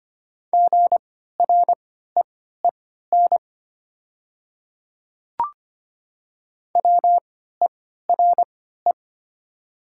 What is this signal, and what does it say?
Morse code, heard as a keyed tone, about 25 wpm, spaced as for 14 wpm, 700 Hz. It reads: GREEN WERE
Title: 3x Character Spacing - Top 500 Words - 25wpm